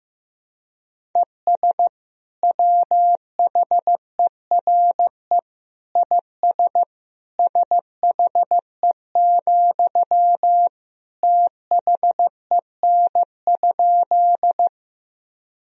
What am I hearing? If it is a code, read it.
Morse code, heard as a keyed tone, about 16 wpm, 700 Hz. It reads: ES WHERE IS SHE, THEN?
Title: Sentences from Top 200 Words and Abbreviations 15wpm